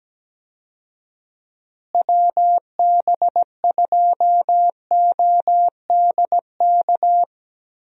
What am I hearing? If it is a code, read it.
Morse code, heard as a keyed tone, about 17 wpm, 700 Hz. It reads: WB2ODK
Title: USA Call Signs - 6 Characters - Repeated 2x - 17wpm